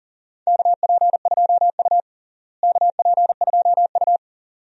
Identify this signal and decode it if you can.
Morse code, heard as a keyed tone, about 40 wpm, 700 Hz. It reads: KP2U KP2U
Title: DXCC-100 Call Signs - 4 Characters - Repeated 2x - 40wpm